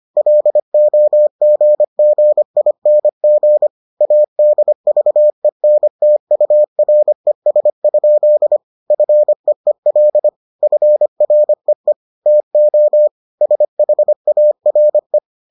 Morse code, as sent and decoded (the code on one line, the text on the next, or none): .-.. --- --. --. .. -. --. / .- -.. ...- . -. - ..- .-. ... ..--.. / ..-. . .-.. / ..-. .-. . / - --- / ... .... .- .-. .
LOGGING ADVENTURES? FEEL FREE TO SHARE